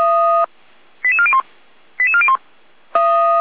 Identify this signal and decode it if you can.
Five-tone selective call (ZVEI), heard as quick runs of five tones, tones of about 70 ms. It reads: 88401 88401